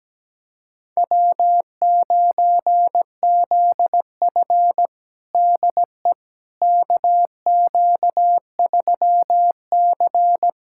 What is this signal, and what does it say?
Morse code, heard as a keyed tone, about 17 wpm, 700 Hz. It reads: W9ZF DE KQ3C
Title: USA Callsign DE Callsign - 4 Characters - 17wpm